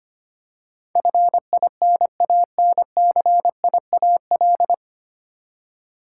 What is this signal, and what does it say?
Morse code, heard as a keyed tone, about 25 wpm, 700 Hz. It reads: FINANCIAL
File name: Nine Letter Words - 25wpm